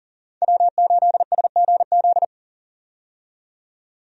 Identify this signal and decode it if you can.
Morse code, heard as a keyed tone, about 40 wpm, 700 Hz. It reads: W8SGZ
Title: USA Call Signs - 5 Characters - 40wpm